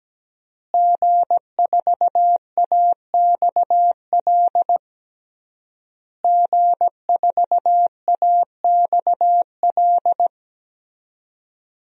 Morse code, heard as a keyed tone, about 17 wpm, 700 Hz. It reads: G4AXL G4AXL
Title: DXCC-100 Call Signs - 5 Characters - Repeated 3x - 17wpm